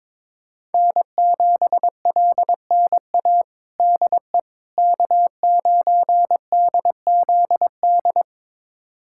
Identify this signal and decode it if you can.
Morse code, heard as a keyed tone, about 22 wpm, 700 Hz. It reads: N7LNA DE K9DZD
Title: USA Callsign DE Callsign - 5 Characters - 22wpm